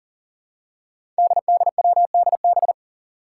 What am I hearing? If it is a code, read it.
Morse code, heard as a keyed tone, about 40 wpm, 700 Hz. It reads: DDWDB
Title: Five Challenging Letter-Number Combinations 40wpm